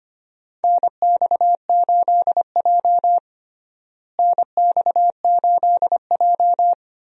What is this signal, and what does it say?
Morse code, heard as a keyed tone, about 25 wpm, 700 Hz. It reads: NX8J NX8J
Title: USA Call Signs - 4 Characters - Repeated 3x - 25wpm